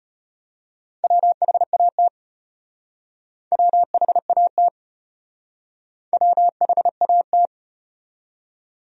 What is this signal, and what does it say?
Morse code, heard as a keyed tone, about 32 wpm, 700 Hz. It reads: WHAT WHAT WHAT